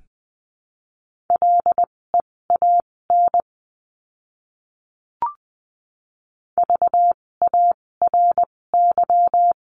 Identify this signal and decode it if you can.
Morse code, heard as a keyed tone, about 20 wpm, spaced as for 16 wpm, 700 Hz. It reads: LEAN VARY
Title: 256 Lesson 42a - Words 12-20wpm